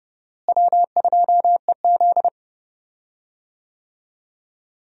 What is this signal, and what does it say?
Morse code, heard as a keyed tone, about 29 wpm, 700 Hz. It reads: W2EZ
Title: USA Call Signs - 4 Characters - Repeated 3x - 30wpm